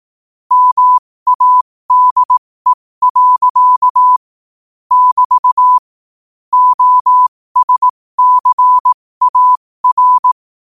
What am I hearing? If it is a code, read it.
Morse code, heard as a keyed tone, about 18 wpm, spaced as for 15 wpm, 1000 Hz. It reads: MADE. <BT> OSCAR
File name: qotdfiles / 15wpm